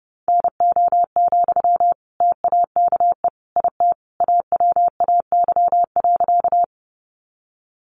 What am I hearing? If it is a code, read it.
Morse code, heard as a keyed tone, about 30 wpm, 700 Hz. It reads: NO, TAKE IT AWAY.
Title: Sentences from Top 600 Words 30wpm